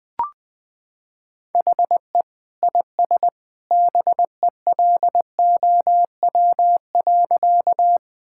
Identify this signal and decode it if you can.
Morse code, heard as a keyed tone, about 19 wpm, 700 Hz. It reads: HE IS BELOW.
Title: Sentences from Top 1000 Words - Review Part 5 - 20wpm